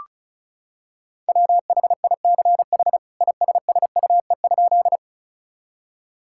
Morse code, heard as a keyed tone, about 36 wpm, 700 Hz. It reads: WHICH ISSUE?
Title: Sentences from Top 1000 Words - Review Part 6 - 35wpm